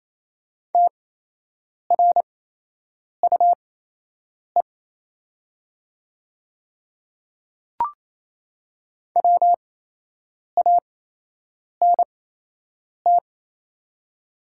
Morse code, heard as a keyed tone, about 28 wpm, spaced as for 8 wpm, 700 Hz. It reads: TRUE WANT
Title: 8x Character Spacing - Top 500 Words - 28wpm